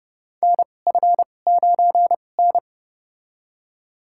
Morse code, heard as a keyed tone, about 30 wpm, spaced as for 22 wpm, 700 Hz. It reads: NF9N
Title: USA Call Signs - 4 Characters - 2x Character Spacing - 30wpm